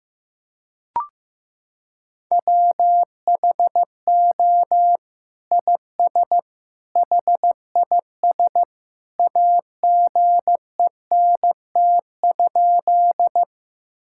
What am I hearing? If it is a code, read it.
Morse code, heard as a keyed tone, about 15 wpm, 700 Hz. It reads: WHO IS HIS AGENT?
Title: Sentences from Top 1600 Words 15wpm